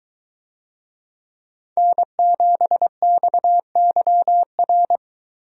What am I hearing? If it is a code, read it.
Morse code, heard as a keyed tone, about 21 wpm, 700 Hz. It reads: N7XYR